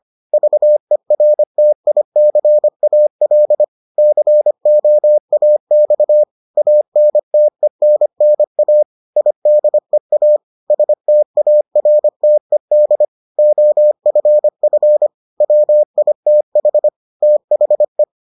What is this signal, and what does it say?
Morse code, heard as a keyed tone, about 25 wpm, 600 Hz. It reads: VERTICAL COAX ANTENNA IDEA STARTED OFF WITH THE